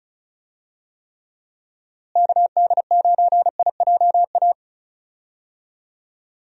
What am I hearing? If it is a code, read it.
Morse code, heard as a keyed tone, about 34 wpm, 700 Hz. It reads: KD9IJA
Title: USA Call Signs - 6 Characters - Repeated 3x - 35wpm